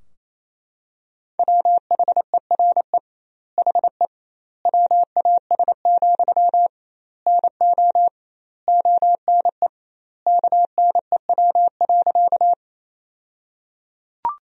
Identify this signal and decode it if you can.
Morse code, heard as a keyed tone, about 28 wpm, 700 Hz. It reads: WHERE HE WAS, NO ONE KNEW.
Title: 2x Word-Spacing - Sentences from Top 400 Words - 28wpm